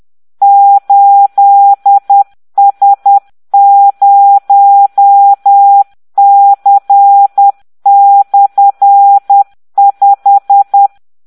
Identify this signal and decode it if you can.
Morse code, heard as a keyed tone, about 10 wpm, 800 Hz. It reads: 8S0C/5